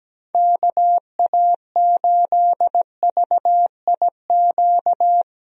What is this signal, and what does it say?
Morse code, heard as a keyed tone, about 17 wpm, 700 Hz. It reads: KA8VIQ
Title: USA Call Signs - 6 Characters - 17wpm